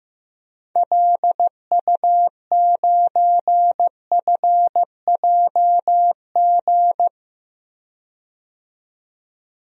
Morse code, heard as a keyed tone, about 15 wpm, 700 Hz. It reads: LU9FJG
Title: DXCC-100 Call Signs - 6 Characters - 15wpm